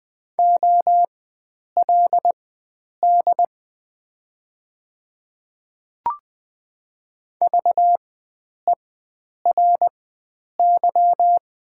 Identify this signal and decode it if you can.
Morse code, heard as a keyed tone, about 20 wpm, spaced as for 9 wpm, 700 Hz. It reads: OLD VERY
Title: 4x Character Spacing - Top 300 Words - 20wpm